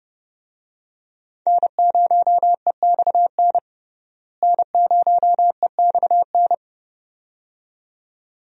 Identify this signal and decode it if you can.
Morse code, heard as a keyed tone, about 30 wpm, 700 Hz. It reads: N0EXN N0EXN
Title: USA Call Signs - 5 Characters - Repeated 2x - 30wpm